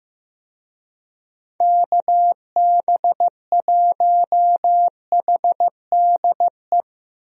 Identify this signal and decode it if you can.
Morse code, heard as a keyed tone, about 16 wpm, 700 Hz. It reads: KB1HDE
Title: USA Call Signs - 6 Characters - 15wpm